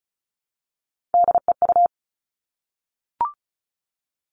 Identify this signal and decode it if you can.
Morse code, heard as a keyed tone, about 34 wpm, 700 Hz. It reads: DEU
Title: All Cut Numbers - 3-Digit 35wpm